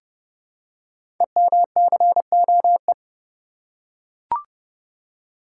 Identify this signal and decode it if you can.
Morse code, heard as a keyed tone, about 30 wpm, 700 Hz. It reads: EMCOE
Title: Code Groups - 5 Letter:Numbers - 30wpm